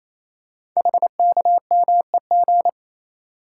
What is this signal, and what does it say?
Morse code, heard as a keyed tone, about 28 wpm, 700 Hz. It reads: HKMEG
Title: Code Groups - 5 Letters - 28wpm